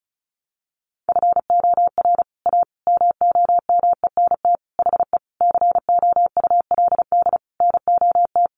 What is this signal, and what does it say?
Morse code, heard as a keyed tone, about 34 wpm, 700 Hz. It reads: FOR A MOMENT HE COULD NOT
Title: Sentences from Top 1100 Words 35wpm